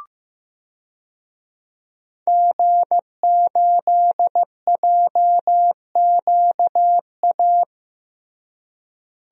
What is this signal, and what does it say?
Morse code, heard as a keyed tone, about 15 wpm, 700 Hz. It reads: G8JQA
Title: DXCC-100 Call Signs - 5 Characters - Repeated 3x - 15wpm